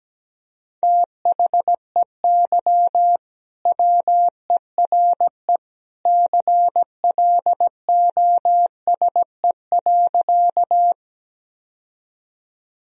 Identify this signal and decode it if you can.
Morse code, heard as a keyed tone, about 17 wpm, 700 Hz. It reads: THEY WERE CLOSE.
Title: Sentences from Top 1000 Words - Review Part 1 - 17wpm